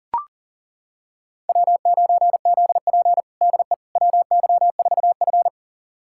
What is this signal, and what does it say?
Morse code, heard as a keyed tone, about 37 wpm, 700 Hz. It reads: W9ZP DE WY4F
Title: USA Callsign DE Callsign - 4 Characters - 40wpm